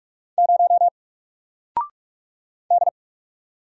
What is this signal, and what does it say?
Morse code, heard as a keyed tone, about 44 wpm, 700 Hz. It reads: 0 D
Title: Single Letter-Number Alt 45wpm